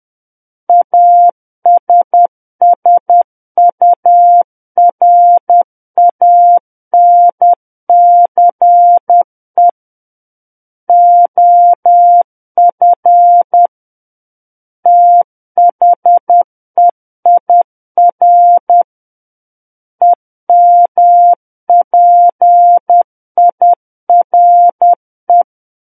Morse code, heard as a keyed tone, about 10 wpm, 700 Hz. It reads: ASSURANCE OF THEIR EMPIRE
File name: Book One / 01-Chapter 1 - 10 WPM